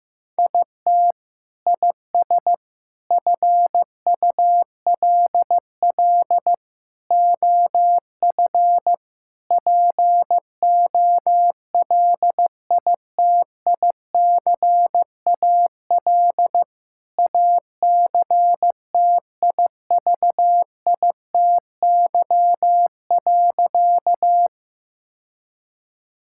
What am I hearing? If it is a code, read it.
Morse code, heard as a keyed tone, about 15 wpm, 700 Hz. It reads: IT IS FULL OF POLITICAL ACTIVITY.
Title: Sentences from Top 1000 Words - Review Part 5 - 15wpm